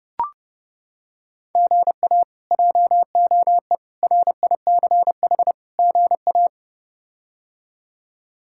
Morse code, heard as a keyed tone, about 30 wpm, 700 Hz. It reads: GA JOE RICH GA
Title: K1USN SST QSO Practice - 30wpm